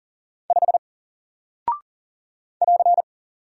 Morse code, heard as a keyed tone, about 40 wpm, 700 Hz. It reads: 5 <AR>
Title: Single Character (Letter:Number:Punctuation:Prosign) Alt 40wpm